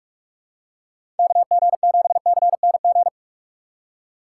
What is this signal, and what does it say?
Morse code, heard as a keyed tone, about 42 wpm, 700 Hz. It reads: KG7CNG
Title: USA Call Signs - 2x Character Spacing - 45wpm